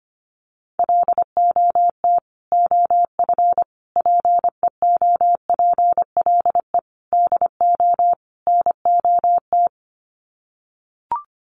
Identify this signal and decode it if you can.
Morse code, heard as a keyed tone, about 25 wpm, 700 Hz. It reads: LOT OF PEOPLE DO NOT
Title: Sets of 5 Words Encore 25wpm